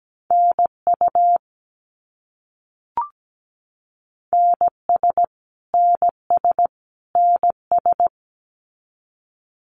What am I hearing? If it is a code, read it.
Morse code, heard as a keyed tone, about 17 wpm, 700 Hz. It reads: NU NS NS NS
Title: Canadian Provinces and Territories - Repeated 3x - 17wpm